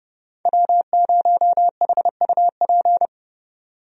Morse code, heard as a keyed tone, about 31 wpm, 700 Hz. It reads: W0HUP
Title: USA Call Signs - 5 Characters - 30wpm